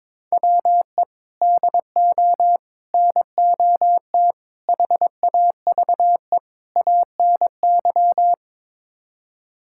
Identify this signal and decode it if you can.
Morse code, heard as a keyed tone, about 22 wpm, 700 Hz. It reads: WE DO NOT HAVE ANY
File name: practice / 22wpm / Build A Phrase - 5-Word Phrases - 22wpm